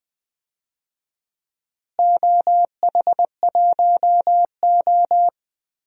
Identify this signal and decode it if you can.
Morse code, heard as a keyed tone, about 20 wpm, 700 Hz. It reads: OH1O
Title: DXCC-100 Call Signs - 4 Characters - Repeated 3x - 20wpm